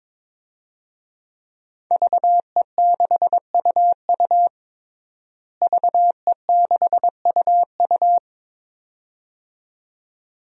Morse code, heard as a keyed tone, about 22 wpm, 700 Hz. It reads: VE6UU VE6UU
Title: DXCC-100 Call Signs - 5 Characters - Repeated 2x - 22wpm